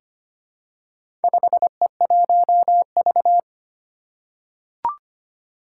Morse code, heard as a keyed tone, about 25 wpm, 700 Hz. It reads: 5E1V
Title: Four Challenging Letter-Number Combinations 25wpm